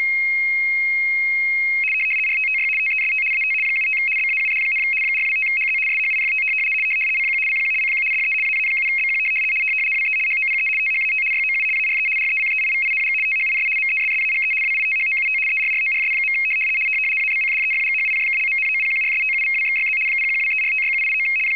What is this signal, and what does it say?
RTTY, about 75 baud. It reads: RTTY TEST AT 75 BAUD, 425HZ SHIFT CQ CQ CQ DE AA7BQ WE HOPE THAT YOU ARE RECEIVING THIS MESSAGE WITHOUT ERRORS. ABCDEFGHIJKLMNOPQRSTUVWXYZ 0123456789
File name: rtty425 / rtty75.wav / rtty75